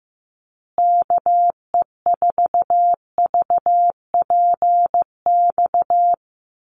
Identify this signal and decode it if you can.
Morse code, heard as a keyed tone, about 15 wpm, 700 Hz. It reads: KE4VPX
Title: USA Call Signs Repeated 2x 15wpm